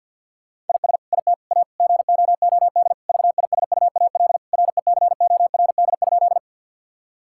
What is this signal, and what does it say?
Morse code, heard as a keyed tone, about 50 wpm, 700 Hz. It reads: IS IT A GOOD VISUAL RECORD?
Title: Sentences from Top 1000 Words - Review Part 6 - 50wpm